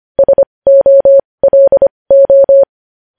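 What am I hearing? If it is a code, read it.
Morse code, heard as a keyed tone, about 25 wpm, spaced as for 20 wpm, 550 Hz. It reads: SOLO
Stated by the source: Morse code news headlines sent at 20.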